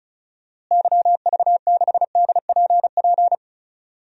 Morse code, heard as a keyed tone, about 36 wpm, 700 Hz. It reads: YV6DPP
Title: DXCC-100 Call Signs - 6 Characters - Repeated 2x - 35wpm